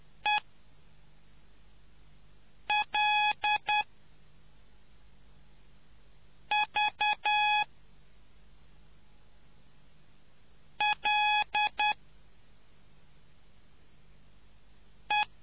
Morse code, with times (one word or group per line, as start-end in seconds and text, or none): 0.2-0.4 E
2.7-3.8 L
6.5-7.6 V
10.8-11.9 L
15.1-15.3 E